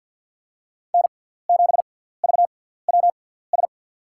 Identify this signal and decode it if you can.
Morse code, heard as a keyed tone, about 46 wpm, spaced as for 17 wpm, 700 Hz. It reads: N7VWS
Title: USA Call Signs - 5 Characters - 6x Character Spacing - 50wpm